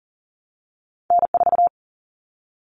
Morse code, heard as a keyed tone, about 40 wpm, 700 Hz. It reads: N4
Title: Callsign Prefixes for Top 25 DXCC - 40wpm